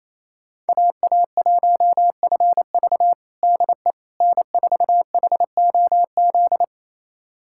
Morse code, heard as a keyed tone, about 28 wpm, 700 Hz. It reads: AA1FV DE N4HOZ
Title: USA Callsign DE Callsign - 5 Characters - 28wpm